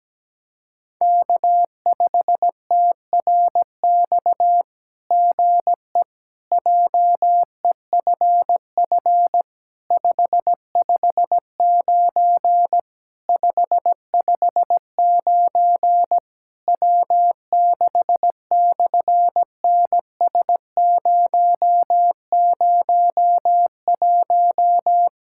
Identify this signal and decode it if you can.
Morse code, heard as a keyed tone, about 17 wpm, 700 Hz. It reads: K5TRX GE JEFF 559 559 W6/NS001
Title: SOTA QSO Practice 17wpm